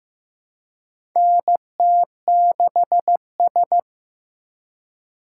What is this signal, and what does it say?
Morse code, heard as a keyed tone, about 15 wpm, 700 Hz. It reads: NT6S
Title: USA Call Signs - 4 Characters - 4x Character Spacing - 15wpm